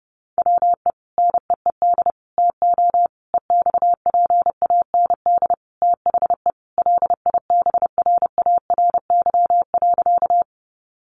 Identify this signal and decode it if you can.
Morse code, heard as a keyed tone, about 30 wpm, 700 Hz. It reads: WE NEED TO EXPAND THE LIBRARY.
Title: Sentences from Top 1400 Words 30wpm